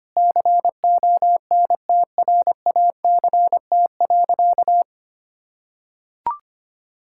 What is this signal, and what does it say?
Morse code, heard as a keyed tone, about 25 wpm, 700 Hz. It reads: CONTRACT.
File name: practice / 25wpm / Sentences with 9-Word Phrases - 25wpm